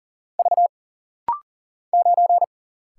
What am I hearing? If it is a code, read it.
Morse code, heard as a keyed tone, about 40 wpm, 700 Hz. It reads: V 9
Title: Single Letter-Number - Rapid-Fire 40wpm